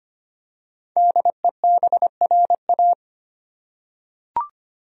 Morse code, heard as a keyed tone, about 25 wpm, 700 Hz. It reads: DEBRA
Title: USA Common Names - 25wpm